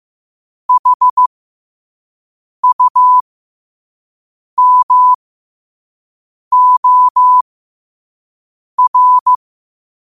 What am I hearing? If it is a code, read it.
Morse code, heard as a keyed tone, about 15 wpm, spaced as for 5 wpm, 1000 Hz. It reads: HUMOR
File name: qotdfiles / 5wpm